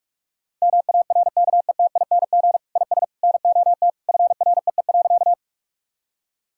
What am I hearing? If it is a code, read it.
Morse code, heard as a keyed tone, about 46 wpm, 700 Hz. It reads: MARKETING IS NOT FREE.
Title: Sentences from Top 1000 Words - Review Part 5 - 45wpm